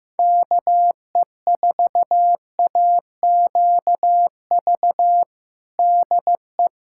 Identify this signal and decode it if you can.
Morse code, heard as a keyed tone, about 15 wpm, 700 Hz. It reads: KE4AQV DE
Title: USA Callsign DE Callsign 15wpm